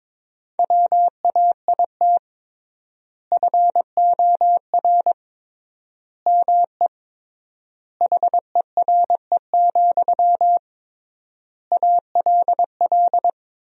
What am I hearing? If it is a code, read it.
Morse code, heard as a keyed tone, about 22 wpm, 700 Hz. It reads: WAIT FOR ME HERE, ALL
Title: 3x Word-Spacing - Sentences from Top 500 Words - 22wpm